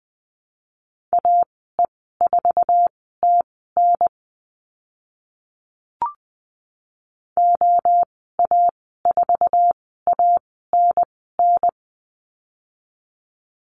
Morse code, heard as a keyed tone, about 20 wpm, spaced as for 14 wpm, 700 Hz. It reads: AE4TN OA4ANN
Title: 055 Lesson 10g - Letter r - Call Signs 20wpm